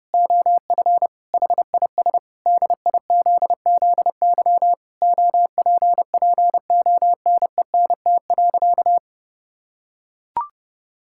Morse code, heard as a keyed tone, about 32 wpm, 700 Hz. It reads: OF HIS DIZZY OPPONENT.
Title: Pangram Challenge v2 - 30wpm